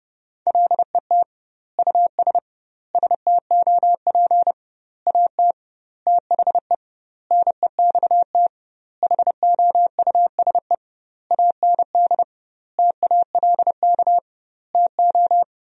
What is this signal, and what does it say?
Morse code, heard as a keyed tone, about 32 wpm, 700 Hz. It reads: LET US STOP AT THE NEXT HOUSE AND TALK TO
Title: 2x Word-Spacing - Sentences from Top 400 Words - 30wpm